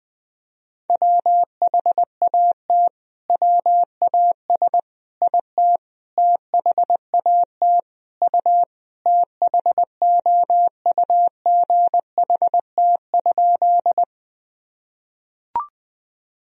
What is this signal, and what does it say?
Morse code, heard as a keyed tone, about 20 wpm, 700 Hz. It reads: WHAT WAS IT THAT U THOUGHT?
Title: Sentences from Top 300 Words and Abbreviations 20wpm